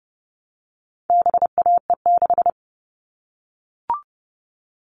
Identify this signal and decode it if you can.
Morse code, heard as a keyed tone, about 30 wpm, 700 Hz. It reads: BAE6
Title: All Cut Numbers - 4-Digit 30wpm